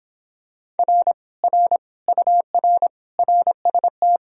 Morse code, heard as a keyed tone, about 26 wpm, 700 Hz. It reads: R R UR RST